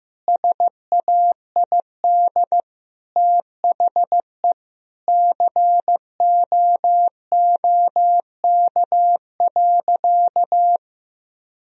Morse code, heard as a keyed tone, about 15 wpm, 700 Hz. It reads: SAID THE COOK.
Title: Sentences from Top 1000 Words - Review Part 4 - 15wpm